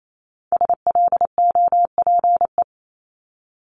Morse code, heard as a keyed tone, about 28 wpm, 700 Hz. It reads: SLOPE
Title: Multi-letter Phonemes Repeated 3x 28wpm